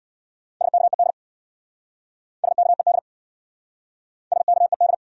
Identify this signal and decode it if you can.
Morse code, heard as a keyed tone, about 68 wpm, 700 Hz. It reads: SKED SKED SKED